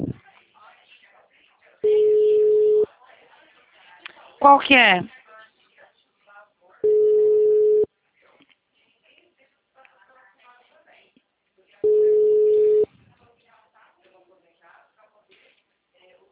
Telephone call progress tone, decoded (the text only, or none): ringback tone